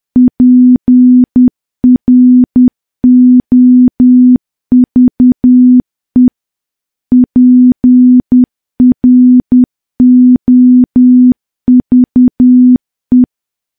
Morse code, heard as a keyed tone, about 10 wpm, 250 Hz. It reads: PROVE PROVE
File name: morse / morse-training-10WPM